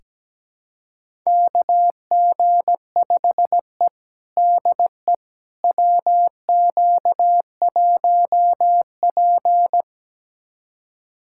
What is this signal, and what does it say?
Morse code, heard as a keyed tone, about 17 wpm, 700 Hz. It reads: KG5E DE WQ1P
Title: USA Callsign DE Callsign - 4 Characters - 17wpm